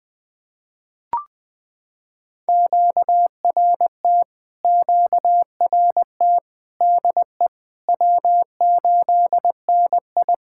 Morse code, heard as a keyed tone, about 21 wpm, 700 Hz. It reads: QRT QRT DE W8NI
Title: POTA QSO Practice - 20wpm